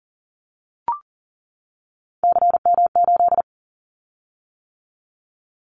Morse code, heard as a keyed tone, about 39 wpm, 700 Hz. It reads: CM8
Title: Callsign Prefixes for Top 100 DXCC - 40wpm